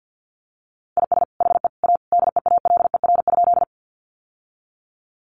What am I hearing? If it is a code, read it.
Morse code, heard as a keyed tone, about 50 wpm, 700 Hz. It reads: IS HE A DEALER?
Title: Sentences from Top 1100 Words 50wpm